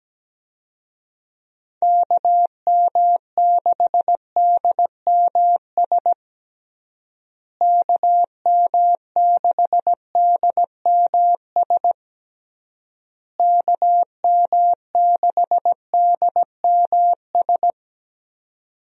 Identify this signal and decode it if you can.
Morse code, heard as a keyed tone, about 17 wpm, 700 Hz. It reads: KM6DMS KM6DMS KM6DMS
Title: USA Call Signs - 6 Characters - Repeated 3x - 17wpm